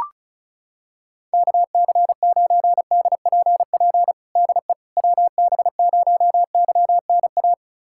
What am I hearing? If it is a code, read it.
Morse code, heard as a keyed tone, about 34 wpm, 700 Hz. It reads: KC9DPP DE WB0YNA
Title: USA Callsign DE Callsign - 6 Characters - 35wpm